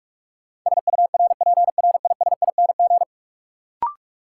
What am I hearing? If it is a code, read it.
Morse code, heard as a keyed tone, about 46 wpm, 700 Hz. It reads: SURPRISING
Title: Sets of 5 Words - New Word Drill - 45wpm